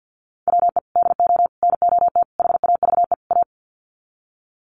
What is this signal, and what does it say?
Morse code, heard as a keyed tone, about 50 wpm, 700 Hz. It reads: WE DO NOT HAVE A
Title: Sets of 5 Words 50wpm